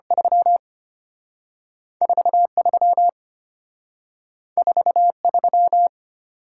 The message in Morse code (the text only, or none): 34 34 3